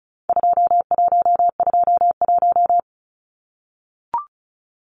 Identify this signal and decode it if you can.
Morse code, heard as a keyed tone, about 34 wpm, 700 Hz. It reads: 2121
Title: Numbers - 4-Digit 35wpm